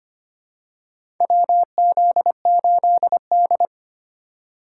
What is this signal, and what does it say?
Morse code, heard as a keyed tone, about 25 wpm, 700 Hz. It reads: WZ8D
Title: USA Call Signs - 4 Characters - 2x Character Spacing - 25wpm